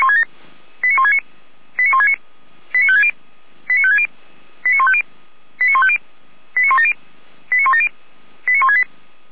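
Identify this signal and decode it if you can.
Five-tone selective call (ZVEI), heard as quick runs of five tones, tones of about 70 ms. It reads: 89169 89169 89560 89560 89150 89150 89170 89170 89168
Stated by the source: ZVEI five-tone call 89223 received by SDR, from audio.f85360.feuerwehr-orlamuende.org fire brigade.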